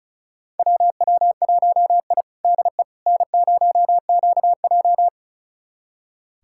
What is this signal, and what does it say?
Morse code, heard as a keyed tone, about 36 wpm, 700 Hz. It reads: WW1I DE N0QJ
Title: USA Callsign DE Callsign - 4 Characters - 35wpm